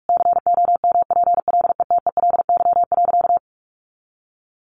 Morse code, heard as a keyed tone, about 44 wpm, 700 Hz. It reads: COMPLETELY.
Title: Sentences from Top 1100 Words 45wpm